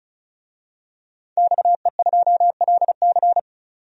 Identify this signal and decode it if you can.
Morse code, heard as a keyed tone, about 36 wpm, 700 Hz. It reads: XE2LC
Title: DXCC-100 Call Signs - 5 Characters - 35wpm